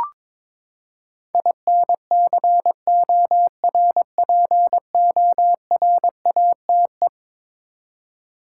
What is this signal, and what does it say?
Morse code, heard as a keyed tone, about 22 wpm, 700 Hz. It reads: INCORPORATE
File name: practice / 22wpm / Eleven Letter Words - 22wpm